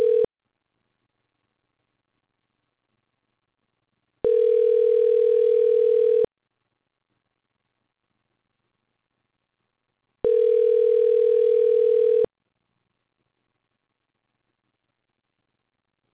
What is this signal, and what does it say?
Ringback tone (North America), 440 + 480 Hz.